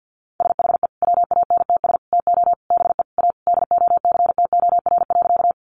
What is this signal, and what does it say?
Morse code, heard as a keyed tone, about 50 wpm, 700 Hz. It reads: SHE WANTS TO BE A DOCTOR.
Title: Easy Phrases 50wpm